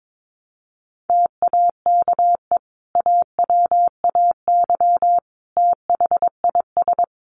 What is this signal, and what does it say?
Morse code, heard as a keyed tone, about 22 wpm, 700 Hz. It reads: TAKE AWAY THIS
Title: Sentences from Top 1300 Words 22wpm